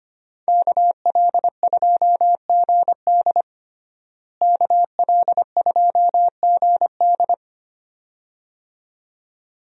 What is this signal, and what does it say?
Morse code, heard as a keyed tone, about 25 wpm, 700 Hz. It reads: KL2GD KL2GD
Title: USA Call Signs - 5 Characters - Repeated 2x - 25wpm